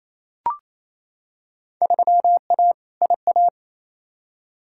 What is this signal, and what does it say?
Morse code, heard as a keyed tone, about 28 wpm, 700 Hz. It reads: 3A IA
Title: Field Day Exchanges Encore - 28wpm